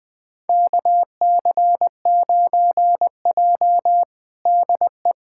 Morse code, heard as a keyed tone, about 20 wpm, 700 Hz. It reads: KC9J DE
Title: USA Callsign DE Callsign - 4 Characters - 20wpm